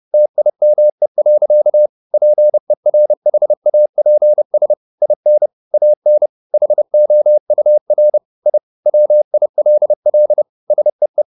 Morse code, heard as a keyed tone, about 30 wpm, 600 Hz. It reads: TIME. PERHAPS IN AN HOUR I WILL SEE